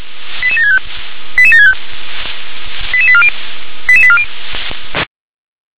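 Five-tone selective call (ZVEI), heard as quick runs of five tones, tones of about 70 ms. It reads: 88765 88765 88840 88840
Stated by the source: ZVEI five-tone selective call 88765 received with an SDR.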